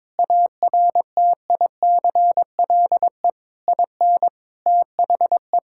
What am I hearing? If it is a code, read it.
Morse code, heard as a keyed tone, about 22 wpm, 700 Hz. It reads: ARTICLE IN THE
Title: Sentences from Top 1000 Words - Review Part 5 - 22wpm